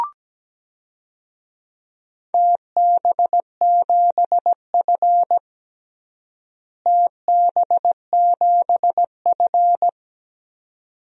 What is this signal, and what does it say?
Morse code, heard as a keyed tone, about 17 wpm, 700 Hz. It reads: TB7F TB7F
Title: DXCC-100 Call Signs - 4 Characters - Repeated 2x - 17wpm